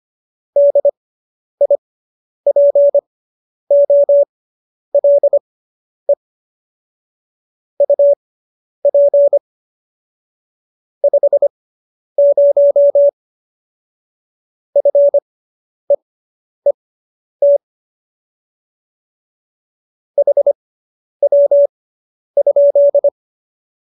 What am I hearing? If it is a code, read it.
Morse code, heard as a keyed tone, about 25 wpm, spaced as for 10 wpm, 570 Hz. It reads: DIPOLE UP 50 FEET HW?